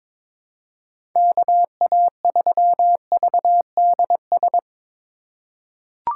Morse code, heard as a keyed tone, about 21 wpm, 700 Hz. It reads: KA3VDS
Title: USA Call Signs - 4x Character Spacing - 22wpm